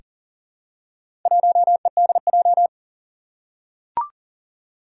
Morse code, heard as a keyed tone, about 43 wpm, 700 Hz. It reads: 1EDJ
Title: Four Challenging Letter-Number Combinations 40wpm